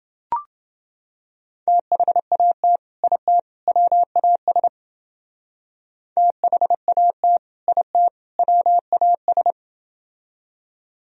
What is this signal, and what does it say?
Morse code, heard as a keyed tone, about 29 wpm, 700 Hz. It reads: THAT IT WAS THAT IT WAS